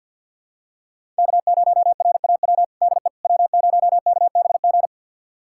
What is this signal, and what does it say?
Morse code, heard as a keyed tone, about 52 wpm, 700 Hz. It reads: K0RAW DE W0KBG
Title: USA Callsign DE Callsign - 5 Characters - 50wpm